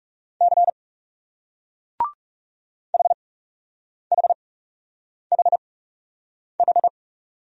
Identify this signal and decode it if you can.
Morse code, heard as a keyed tone, about 39 wpm, spaced as for 8 wpm, 700 Hz. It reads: C HHHH